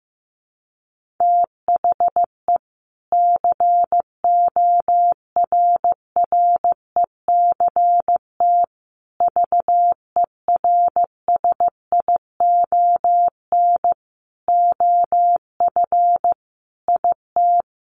Morse code, heard as a keyed tone, about 15 wpm, 700 Hz. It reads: THE CORRECT VERSION OF IT
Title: Sentences from Top 1400 Words 15wpm